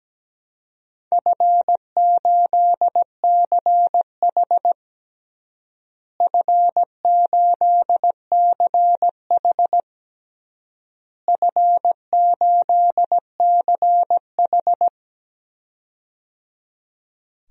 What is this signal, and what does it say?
Morse code, heard as a keyed tone, about 17 wpm, 700 Hz. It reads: F8CH F8CH F8CH
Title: DXCC-100 Call Signs - 4 Characters - Repeated 3x - 17wpm